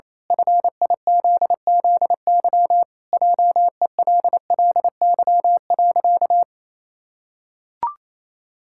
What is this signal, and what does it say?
Morse code, heard as a keyed tone, about 27 wpm, 700 Hz. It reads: FIZZY JELLY.
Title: Pangram Challenge v2 - 28wpm